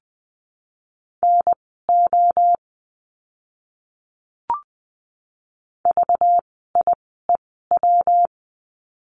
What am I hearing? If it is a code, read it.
Morse code, heard as a keyed tone, about 20 wpm, spaced as for 15 wpm, 700 Hz. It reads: NO VIEW
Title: 192 Lesson 31f - Letter v - Words 20wpm